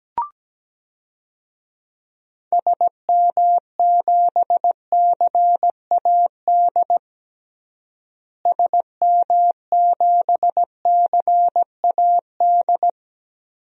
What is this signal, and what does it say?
Morse code, heard as a keyed tone, about 17 wpm, 700 Hz. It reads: SM7CAD SM7CAD
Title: DXCC-100 Call Signs - 6 Characters - Repeated 2x - 17wpm